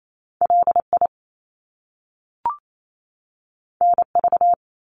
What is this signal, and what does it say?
Morse code, heard as a keyed tone, about 28 wpm, 700 Hz. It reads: LI NV
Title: Top Two Letter Combinations in Words 28wpm